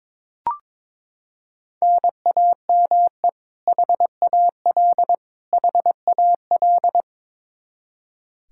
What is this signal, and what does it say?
Morse code, heard as a keyed tone, about 22 wpm, 700 Hz. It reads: NAME HAL HAL
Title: Practice QSO - Exchange 1 - Beginning - 22wpm